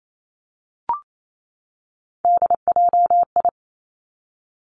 Morse code, heard as a keyed tone, about 28 wpm, 700 Hz. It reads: DJI
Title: Three Challenging Letter-Number Combinations 28wpm